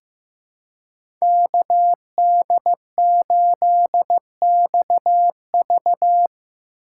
Morse code, heard as a keyed tone, about 15 wpm, 700 Hz. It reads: KD8XV
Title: USA Call Signs - 5 Characters - 3x Character Spacing - 15wpm